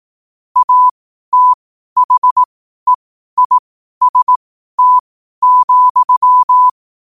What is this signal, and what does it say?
Morse code, heard as a keyed tone, about 18 wpm, spaced as for 12 wpm, 1000 Hz. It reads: ATHEIST,